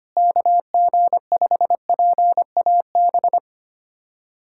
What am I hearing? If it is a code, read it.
Morse code, heard as a keyed tone, about 25 wpm, 700 Hz. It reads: KG5PAB
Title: USA Callsign DE Callsign - 6 Characters - 25wpm